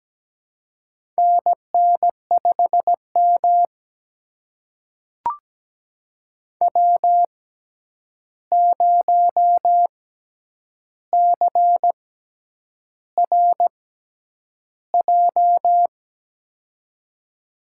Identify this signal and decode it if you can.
Morse code, heard as a keyed tone, about 17 wpm, spaced as for 8 wpm, 700 Hz. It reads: NN5M W0CRJ
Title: USA Call Signs - 6x Character Spacing - 17wpm